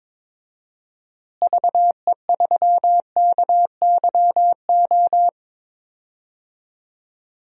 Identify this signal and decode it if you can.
Morse code, heard as a keyed tone, about 22 wpm, 700 Hz. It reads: VE3KYO
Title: DXCC-100 Call Signs - 6 Characters - Repeated 2x - 22wpm